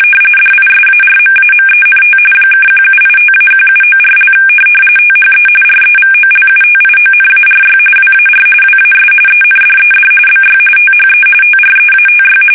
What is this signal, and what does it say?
RTTY, about 45.45 baud. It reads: DISARMAMENT (1785-1794) THE REVOLUTIONARY WAR WAS ENDED BY THE TREATY
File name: fsk850 / usnavy1